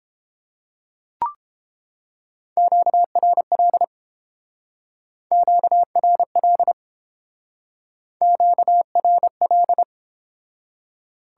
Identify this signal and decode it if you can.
Morse code, heard as a keyed tone, about 30 wpm, 700 Hz. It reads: QRL QRL QRL